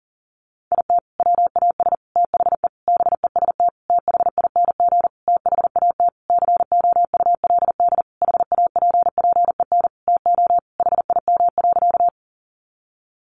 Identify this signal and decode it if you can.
Morse code, heard as a keyed tone, about 37 wpm, 700 Hz. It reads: IT WAS THE BEST THING THAT COULD HAPPEN TO HIM.
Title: Sentences from Top 400 Words 40wpm